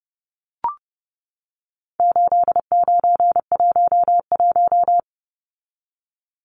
Morse code, heard as a keyed tone, about 30 wpm, 700 Hz. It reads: 8911
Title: Numbers - 4-Digit 30wpm